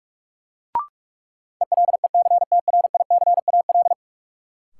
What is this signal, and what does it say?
Morse code, heard as a keyed tone, about 42 wpm, 700 Hz. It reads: ELECTRICAL
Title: Sentences from Top 1000 Words - Review Part 6 - New Word Drill - 45wpm